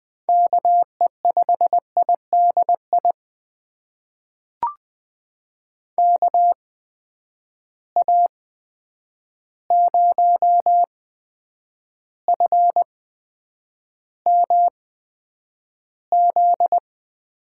Morse code, heard as a keyed tone, about 20 wpm, spaced as for 9 wpm, 700 Hz. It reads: KE5IDI KA0FMZ
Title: USA Call Signs - 6 Characters - 8x Character Spacing - 20wpm